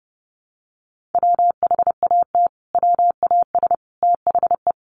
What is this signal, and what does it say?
Morse code, heard as a keyed tone, about 30 wpm, 700 Hz. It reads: WHAT WAS THE
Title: Sentences from Top 1400 Words 30wpm